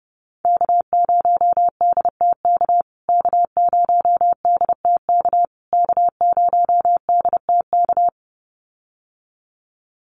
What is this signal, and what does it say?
Morse code, heard as a keyed tone, about 30 wpm, 700 Hz. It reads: K0DTK K0DTK K0DTK
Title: USA Call Signs Repeated 3x 30wpm